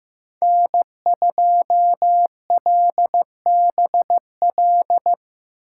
Morse code, heard as a keyed tone, about 15 wpm, 700 Hz. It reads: N2LBL
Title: USA Call Signs - 5 Characters - 4x Character Spacing - 15wpm